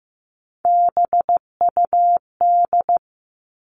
Morse code, heard as a keyed tone, about 15 wpm, 700 Hz. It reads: BUD
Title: All Cut Numbers - 3-Digit 15wpm